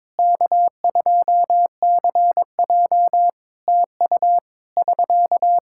Morse code, heard as a keyed tone, about 22 wpm, 700 Hz. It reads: K2CJ TU <SK>
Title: K1USN SST QSO Practice 22wpm